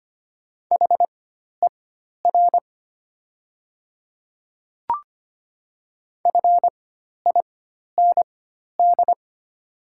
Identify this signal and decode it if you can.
Morse code, heard as a keyed tone, about 25 wpm, spaced as for 12 wpm, 700 Hz. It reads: HER FIND
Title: 4x Character Spacing - Top 100 Words - 25wpm